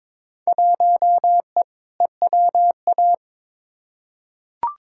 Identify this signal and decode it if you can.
Morse code, heard as a keyed tone, about 22 wpm, 700 Hz. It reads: 1E EWA
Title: Field Day Exchanges - 22wpm